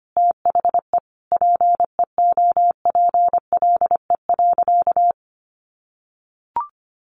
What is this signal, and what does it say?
Morse code, heard as a keyed tone, about 25 wpm, 700 Hz. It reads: THE PEOPLE.
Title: Sentences from Top 1500 Words 25wpm